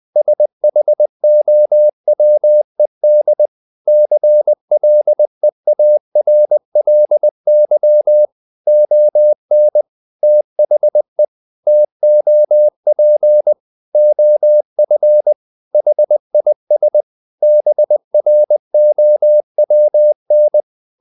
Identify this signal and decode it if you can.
Morse code, heard as a keyed tone, about 20 wpm, 600 Hz. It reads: SHOWED CLEARLY ON THE TOP OF HIS BROWN
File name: the old man and the sea part 2 20wpm